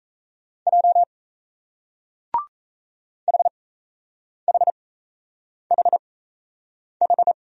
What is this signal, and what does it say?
Morse code, heard as a keyed tone, about 38 wpm, spaced as for 8 wpm, 700 Hz. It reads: J HHHH